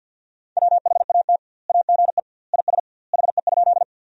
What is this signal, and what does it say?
Morse code, heard as a keyed tone, about 50 wpm, 700 Hz. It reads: WHAT AGE IS HE?
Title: 2x Word-Spacing - Sentences from Top 500 Words - 50wpm